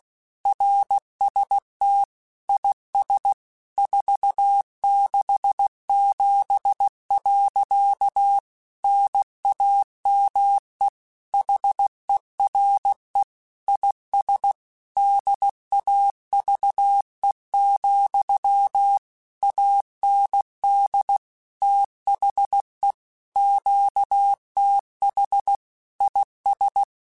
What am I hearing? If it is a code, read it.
Morse code, heard as a keyed tone, about 16 wpm, 800 Hz. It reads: RST IS 467. NAME HERE IS DAVE, AND THE QTH IS
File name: QSOs / QSOs-mp3s / kc7haa@15